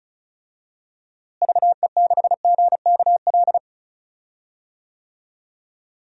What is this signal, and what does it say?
Morse code, heard as a keyed tone, about 34 wpm, 700 Hz. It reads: VE6GKL
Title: DXCC-100 Call Signs - 6 Characters - Repeated 2x - 35wpm